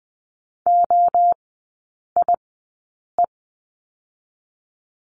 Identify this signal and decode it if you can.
Morse code, heard as a keyed tone, about 20 wpm, 700 Hz. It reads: O I E
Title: 029 Lesson 6e - Letter i - Three Characters 20wpm